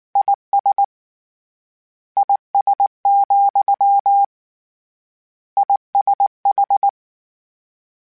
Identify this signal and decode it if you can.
Morse code, heard as a keyed tone, about 19 wpm, 800 Hz. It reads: IS IS, ISH